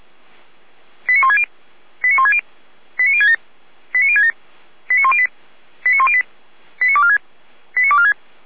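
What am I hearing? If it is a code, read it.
Five-tone selective call (ZVEI), heard as quick runs of five tones, tones of about 70 ms. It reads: 89160 89160 89067 89067 89108 89108 89246 89246
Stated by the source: ZVEI five-tone call 89150 received by SDR.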